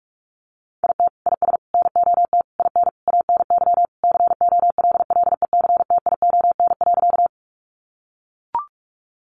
Morse code, heard as a keyed tone, about 44 wpm, 700 Hz. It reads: IT IS NOT IN ANY COLLECTION.
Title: Sentences from Top 1200 Words 45wpm